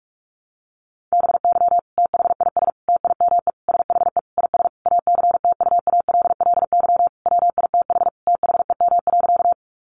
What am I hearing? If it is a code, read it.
Morse code, heard as a keyed tone, about 44 wpm, 700 Hz. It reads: BY THIS TIME SHE IS ACTUALLY WITH THEM.
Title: Sentences from Top 700 Words 45wpm